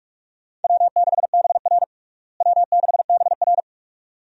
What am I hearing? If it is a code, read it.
Morse code, heard as a keyed tone, about 46 wpm, 700 Hz. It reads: W6BR W6BR
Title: USA Call Signs - 4 Characters - Repeated 2x - 45wpm